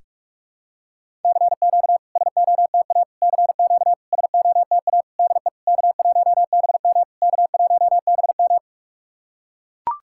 Morse code, heard as a keyed tone, about 44 wpm, 660 Hz. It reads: CQ SOTA CQ SOTA DE K1BM K1BM
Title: SOTA QSO Practice 45wpm